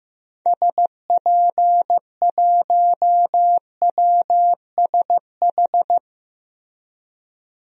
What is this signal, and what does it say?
Morse code, heard as a keyed tone, about 15 wpm, 700 Hz. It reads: SP1WSH
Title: DXCC-100 Call Signs - 6 Characters - Repeated 3x - 15wpm